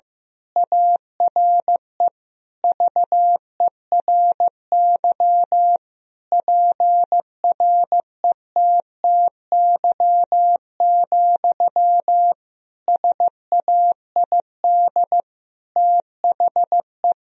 Morse code, heard as a keyed tone, about 16 wpm, 700 Hz. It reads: ARE VERY PRETTY, SAID THE
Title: Sentences from Top 1000 Words - Review Part 4 - 15wpm